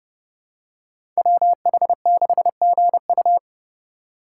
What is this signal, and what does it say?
Morse code, heard as a keyed tone, about 32 wpm, 700 Hz. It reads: WH6GU
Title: USA Call Signs - 5 Characters - 8x Character Spacing - 30wpm